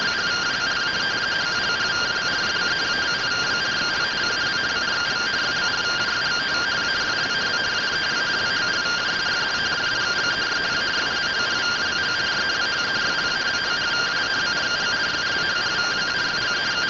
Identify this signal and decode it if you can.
Two-tone teleprinter signal, about 170 Hz shift.